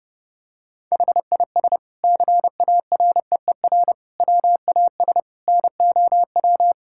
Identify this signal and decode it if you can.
Morse code, heard as a keyed tone, about 30 wpm, 700 Hz. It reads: HIS CAREER WAS NOW
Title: Sentences from Top 1000 Words - Review Part 6 - 30wpm